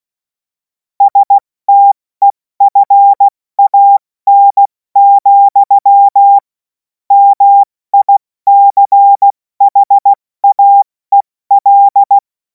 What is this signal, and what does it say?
Morse code, heard as a keyed tone, about 16 wpm, spaced as for 14 wpm, 800 Hz. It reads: STEFAN, MICHAEL
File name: Morsekurs / Witze / Witz 07. April 2026 70 F80